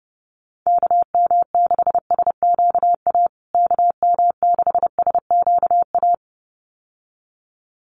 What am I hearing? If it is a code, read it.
Morse code, heard as a keyed tone, about 30 wpm, 700 Hz. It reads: KM6SQA KM6SQA
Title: USA Call Signs Repeated 2x 30wpm